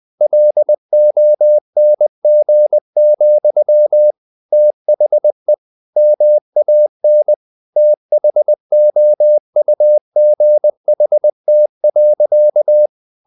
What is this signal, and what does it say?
Morse code, heard as a keyed tone, about 20 wpm, 600 Hz. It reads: LONG, THE MAN THOUGHT.
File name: the old man and the sea / the old man and the sea part 1 20wpm